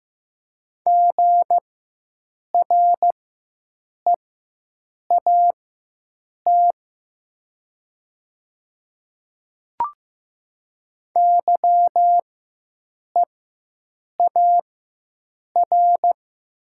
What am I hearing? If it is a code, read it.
Morse code, heard as a keyed tone, about 15 wpm, spaced as for 7 wpm, 700 Hz. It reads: GREAT YEAR
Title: 4x Character Spacing - Top 300 Words - 15wpm